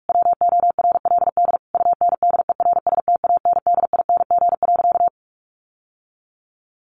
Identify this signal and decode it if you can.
Morse code, heard as a keyed tone, about 44 wpm, 700 Hz. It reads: WORLD UNDERSTANDING.
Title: Sentences from Top 1400 Words 45wpm